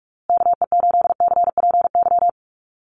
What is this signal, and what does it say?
Morse code, heard as a keyed tone, about 44 wpm, 690 Hz. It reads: KE8CPY
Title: USA Call Signs Repeated 3x 45wpm